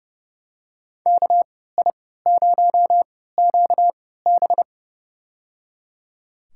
Morse code, heard as a keyed tone, about 30 wpm, spaced as for 17 wpm, 700 Hz. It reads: KI0QB
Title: USA Call Signs - 5 Characters - 3x Character Spacing - 30wpm